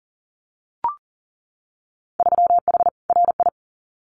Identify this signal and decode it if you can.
Morse code, heard as a keyed tone, about 40 wpm, 700 Hz. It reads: 3H RI
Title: Winter Field Day Exchanges Encore 40wpm